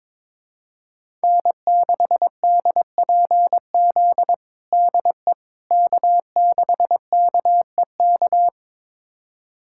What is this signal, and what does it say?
Morse code, heard as a keyed tone, about 21 wpm, 700 Hz. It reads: N6DPZ DE K6KEK
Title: USA Callsign DE Callsign - 5 Characters - 22wpm